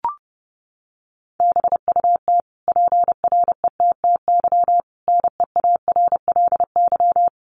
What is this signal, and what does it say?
Morse code, heard as a keyed tone, about 30 wpm, 700 Hz. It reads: BUT PRETTY NEARLY
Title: Sentences from Top 1300 Words 30wpm